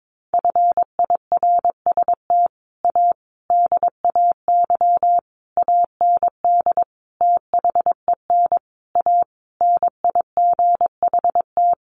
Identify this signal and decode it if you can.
Morse code, heard as a keyed tone, about 22 wpm, 700 Hz. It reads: FIRST A DAY AND THEN A NIGHT
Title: A Sound Of Thunder 22wpm